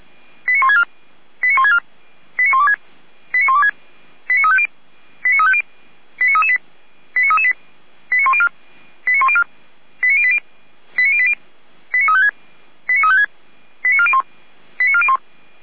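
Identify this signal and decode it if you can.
Five-tone selective call (ZVEI), heard as quick runs of five tones, tones of about 70 ms. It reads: 89164 89164 89126 89126 89350 89350 89338 89338 89104 89104 89080 89080 89356 89356 89401 89401
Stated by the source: ZVEI five-tone selective call 89701, received by SDR.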